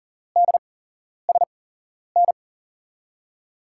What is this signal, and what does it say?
Morse code, heard as a keyed tone, about 40 wpm, spaced as for 11 wpm, 700 Hz. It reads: DSN
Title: USA Call Signs - 8x Character Spacing - 40wpm